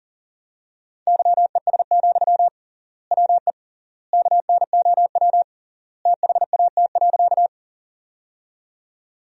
Morse code, heard as a keyed tone, about 37 wpm, 700 Hz. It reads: YES, WE KNOW THAT.
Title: 3x Word-Spacing - Sentences from Top 500 Words - 40wpm